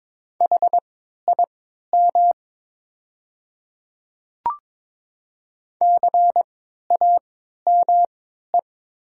Morse code, heard as a keyed tone, about 23 wpm, spaced as for 13 wpm, 700 Hz. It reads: HIM CAME
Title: 3x Character Spacing - Top 300 Words - 22wpm